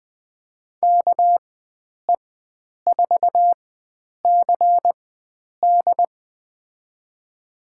Morse code, heard as a keyed tone, about 20 wpm, spaced as for 9 wpm, 700 Hz. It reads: KE4CD
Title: USA Call Signs - 5 Characters - 4x Character Spacing - 20wpm